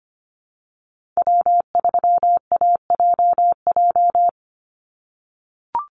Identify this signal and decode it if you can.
Morse code, heard as a keyed tone, about 25 wpm, 700 Hz. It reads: W3AJJ
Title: USA Call Signs 25wpm